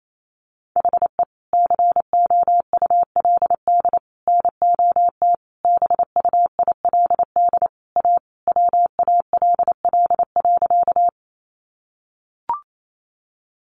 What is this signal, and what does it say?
Morse code, heard as a keyed tone, about 28 wpm, 700 Hz. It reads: HE COULD NOT BUILD A WALL.
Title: Sentences from Top 1100 Words 28wpm